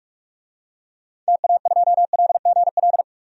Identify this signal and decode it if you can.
Morse code, heard as a keyed tone, about 44 wpm, 700 Hz. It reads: TA2LGL
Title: DXCC-100 Call Signs - 6 Characters - 45wpm